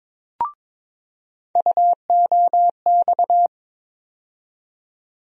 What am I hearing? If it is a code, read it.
Morse code, heard as a keyed tone, about 22 wpm, 700 Hz. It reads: UOX
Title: USA Call Sign Postfixes - 3 Letters 22wpm